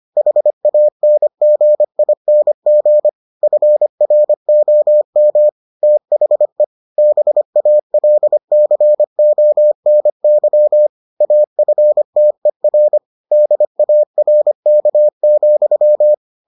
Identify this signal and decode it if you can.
Morse code, heard as a keyed tone, about 25 wpm, 600 Hz. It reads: HANGING FROM THE BALCONY AFTER DARK,